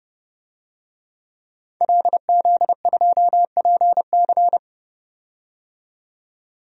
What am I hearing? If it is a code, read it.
Morse code, heard as a keyed tone, about 30 wpm, 700 Hz. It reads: LZ2PC
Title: DXCC-100 Call Signs - 5 Characters - 30wpm